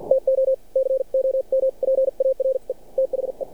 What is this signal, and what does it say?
Morse code, heard as a keyed tone, about 53 wpm, spaced as for 44 wpm, 530 Hz. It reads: TO COMPARE THE
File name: archived / CRSnet